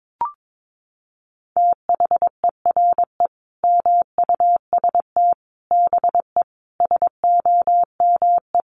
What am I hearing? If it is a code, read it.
Morse code, heard as a keyed tone, about 22 wpm, 700 Hz. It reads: THERE MUST BE SOME
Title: Sentences from Top 700 Words 22wpm